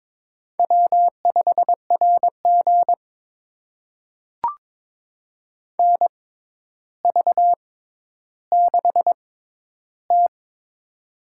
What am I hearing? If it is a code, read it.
Morse code, heard as a keyed tone, about 22 wpm, spaced as for 11 wpm, 700 Hz. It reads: W5RG NV6T
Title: USA Call Signs - 4 Characters - 6x Character Spacing - 22wpm